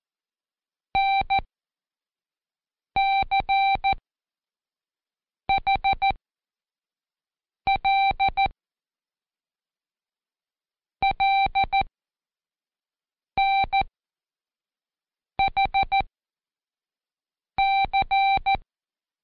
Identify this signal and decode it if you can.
Morse code, heard as a keyed tone, about 14 wpm, spaced as for 5 wpm, 780 Hz. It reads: NCHL LNHC
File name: Code Quick / AudioDownld1 / Seg37-Day4-Quiz CHLN